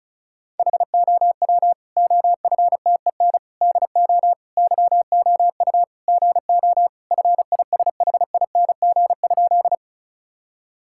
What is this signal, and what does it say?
Morse code, heard as a keyed tone, about 34 wpm, 700 Hz. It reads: HOW OFTEN DO YOU GO FISHING?
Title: Sentences from Top 1000 Words - Review Part 5 - 35wpm